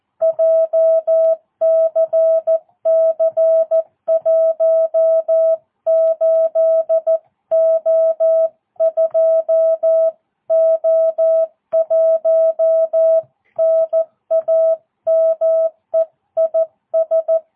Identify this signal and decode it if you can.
Morse code, heard as a keyed tone, about 14 wpm, 650 Hz. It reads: JCC18O2O1 NAME IS
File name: jh8cbh / 2022cw